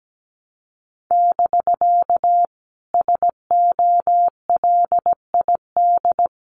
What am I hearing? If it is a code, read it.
Morse code, heard as a keyed tone, about 17 wpm, 700 Hz. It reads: <BK> SOLID
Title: Common QSO Phrases Encore 17wpm